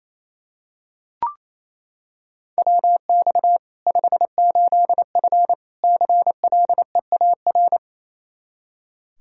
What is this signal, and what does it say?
Morse code, heard as a keyed tone, about 28 wpm, 700 Hz. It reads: WX 58F CLEAR
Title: Practice QSO - Exchange 2 - Info - 28wpm